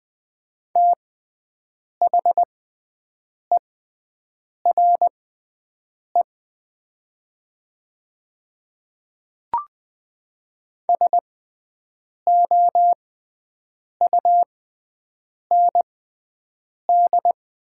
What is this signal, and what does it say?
Morse code, heard as a keyed tone, about 20 wpm, spaced as for 7 wpm, 700 Hz. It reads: THERE SOUND
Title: 6x Character Spacing - Top 300 Words - 20wpm